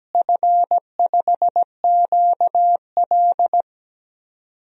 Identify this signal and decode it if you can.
Morse code, heard as a keyed tone, about 17 wpm, 700 Hz. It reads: F5QL
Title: DXCC-100 Call Signs - 4 Characters - Repeated 3x - 17wpm